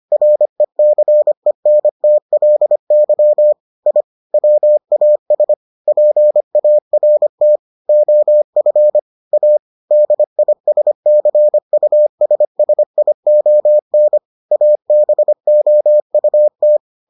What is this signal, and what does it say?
Morse code, heard as a keyed tone, about 25 wpm, 600 Hz. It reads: RECENTLY I WAS PART OF A DISCUSSION ABOUT